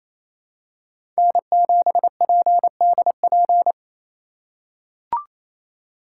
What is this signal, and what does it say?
Morse code, heard as a keyed tone, about 28 wpm, 700 Hz. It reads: N7PDP
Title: USA Call Signs - 5 Characters - 2x Character Spacing - 28wpm